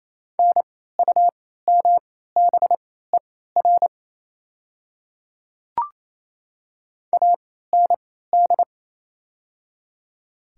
Morse code, heard as a keyed tone, about 28 wpm, spaced as for 16 wpm, 700 Hz. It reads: NUMBER AND
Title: 3x Character Spacing - Top 100 Words - 28wpm